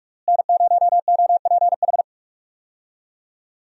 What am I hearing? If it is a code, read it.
Morse code, heard as a keyed tone, about 46 wpm, 700 Hz. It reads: N0OPH
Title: USA Call Signs - 5 Characters - Repeated 3x - 45wpm